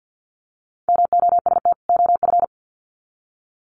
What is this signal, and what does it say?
Morse code, heard as a keyed tone, about 50 wpm, 700 Hz. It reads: MOST OF
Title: Sets of 2 Words 50wpm